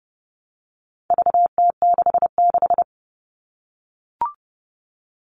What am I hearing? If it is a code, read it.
Morse code, heard as a keyed tone, about 30 wpm, 700 Hz. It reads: VT66
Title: All Cut Numbers - 4-Digit 30wpm